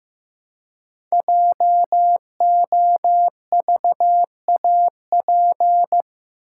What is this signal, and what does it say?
Morse code, heard as a keyed tone, about 15 wpm, 700 Hz. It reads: JOVAP
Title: Code Groups - 5 Letters - 15wpm